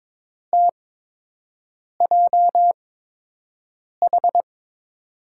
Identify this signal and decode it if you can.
Morse code, heard as a keyed tone, about 22 wpm, spaced as for 6 wpm, 700 Hz. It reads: TJH
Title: USA Call Signs - 8x Character Spacing - 22wpm